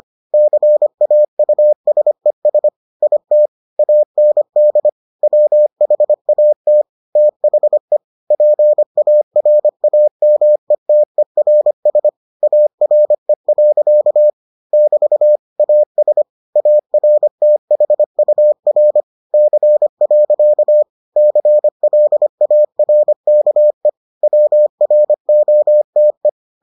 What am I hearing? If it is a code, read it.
Morse code, heard as a keyed tone, about 25 wpm, 600 Hz. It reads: CAUSES IT AND WHAT THE PARAMETERS ARE. <BT> AS ARTHUR C. CLARKE WROTE